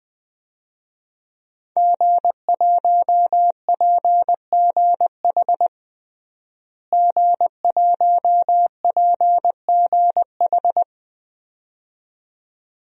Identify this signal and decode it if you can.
Morse code, heard as a keyed tone, about 20 wpm, 700 Hz. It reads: G1PGH G1PGH
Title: DXCC-100 Call Signs - 5 Characters - Repeated 2x - 20wpm